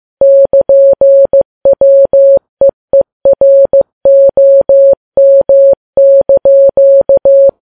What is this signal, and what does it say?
Morse code, heard as a keyed tone, about 15 wpm, 550 Hz. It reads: <KN>WEEROM)
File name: Files Web tls / Morse on9fv / morse15kps